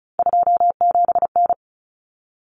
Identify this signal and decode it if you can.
Morse code, heard as a keyed tone, about 34 wpm, 700 Hz. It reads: 27N
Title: Common Cut Numbers - 3-Digit 35wpm